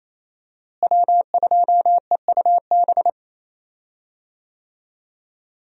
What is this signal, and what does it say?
Morse code, heard as a keyed tone, about 28 wpm, 700 Hz. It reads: W2EUB
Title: USA Call Signs - 5 Characters - Repeated 3x - 28wpm